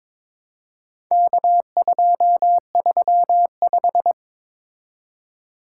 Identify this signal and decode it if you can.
Morse code, heard as a keyed tone, about 22 wpm, 700 Hz. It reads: K235
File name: practice / 22wpm / Four Letter-Number Combinations - 22wpm